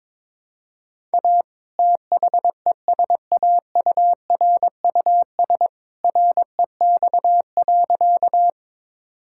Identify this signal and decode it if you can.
Morse code, heard as a keyed tone, about 21 wpm, 700 Hz. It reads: A THESAURUS REX.
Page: morse-code-ninja-practice-22wpm